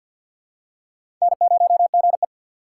Morse code, heard as a keyed tone, about 53 wpm, 700 Hz. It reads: N0GE
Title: USA Call Signs - 4 Characters - 3x Character Spacing - 50wpm